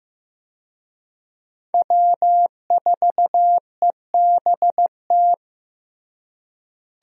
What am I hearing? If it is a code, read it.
Morse code, heard as a keyed tone, about 15 wpm, 700 Hz. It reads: W4EBT
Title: USA Call Signs - 5 Characters - Repeated 3x - 15wpm